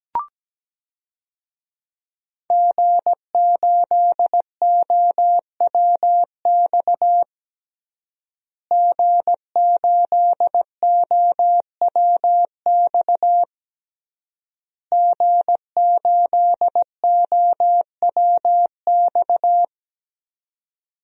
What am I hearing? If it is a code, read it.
Morse code, heard as a keyed tone, about 17 wpm, 700 Hz. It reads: G8OWX G8OWX G8OWX
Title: DXCC-100 Call Signs - 5 Characters - Repeated 3x - 17wpm